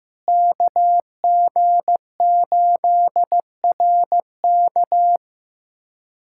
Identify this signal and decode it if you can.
Morse code, heard as a keyed tone, about 15 wpm, 700 Hz. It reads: KG8RK
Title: USA Callsign DE Callsign 15wpm